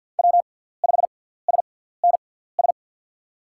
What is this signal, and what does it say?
Morse code, heard as a keyed tone, about 50 wpm, spaced as for 17 wpm, 700 Hz. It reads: W5SNS
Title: USA Call Signs - 5 Characters - 6x Character Spacing - 50wpm